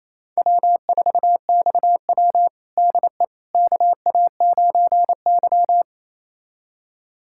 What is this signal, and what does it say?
Morse code, heard as a keyed tone, about 28 wpm, 700 Hz. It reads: W4XW DE KA9Y
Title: USA Callsign DE Callsign - 4 Characters - 28wpm